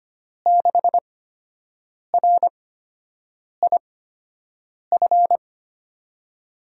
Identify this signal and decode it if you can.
Morse code, heard as a keyed tone, about 25 wpm, spaced as for 7 wpm, 700 Hz. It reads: 6RIF
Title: USA Call Signs - 8x Character Spacing - 25wpm